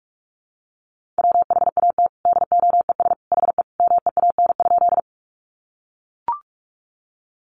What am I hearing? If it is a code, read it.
Morse code, heard as a keyed tone, about 44 wpm, 700 Hz. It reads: WHAT DOES HE MEAN?
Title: Sentences from Top 200 Words 45wpm